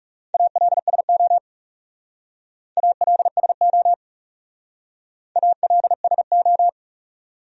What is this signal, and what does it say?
Morse code, heard as a keyed tone, about 38 wpm, 700 Hz. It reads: ALSO ALSO ALSO
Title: Speed-Racing Top 200 Words 30wpm